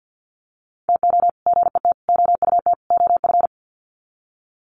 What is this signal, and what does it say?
Morse code, heard as a keyed tone, about 50 wpm, 700 Hz. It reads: TO GET OUT OF
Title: Sets of 4 Words 50wpm